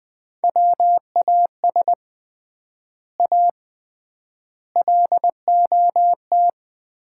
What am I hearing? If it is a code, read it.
Morse code, heard as a keyed tone, about 20 wpm, 700 Hz. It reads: WAS A LOT
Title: 3x Word-Spacing - Sets of 4 Words - 20wpm